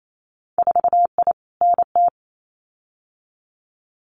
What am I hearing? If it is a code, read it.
Morse code, heard as a keyed tone, about 28 wpm, 700 Hz. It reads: 4I NT
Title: Winter Field Day Exchanges Encore 28wpm